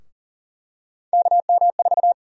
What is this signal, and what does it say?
Morse code, heard as a keyed tone, about 41 wpm, 700 Hz. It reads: KM4
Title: USA Call Sign Prefixes - Repeated 2x - 40wpm